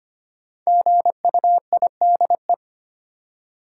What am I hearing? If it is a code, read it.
Morse code, heard as a keyed tone, about 25 wpm, 700 Hz. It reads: GUIDE
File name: 5 Letter Words - First 3 Letters Given - 25wpm